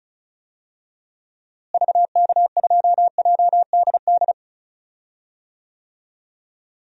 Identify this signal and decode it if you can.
Morse code, heard as a keyed tone, about 34 wpm, 700 Hz. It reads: VK2JDD
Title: DXCC-100 Call Signs - 6 Characters - Repeated 2x - 35wpm